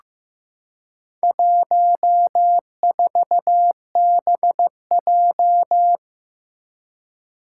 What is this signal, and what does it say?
Morse code, heard as a keyed tone, about 15 wpm, 700 Hz. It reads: 14BJ
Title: Four Challenging Letter-Number Combinations 15wpm